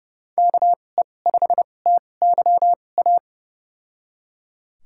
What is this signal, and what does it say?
Morse code, heard as a keyed tone, about 30 wpm, spaced as for 22 wpm, 700 Hz. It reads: KE5TYA
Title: USA Call Signs - 6 Characters - 2x Character Spacing - 30wpm